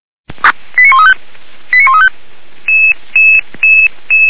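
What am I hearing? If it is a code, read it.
Five-tone selective call (ZVEI), heard as quick runs of five tones, tones of about 70 ms. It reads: 89135 89135
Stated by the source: Selective call received by SDR, logged by Feuerwehr Orlamünde.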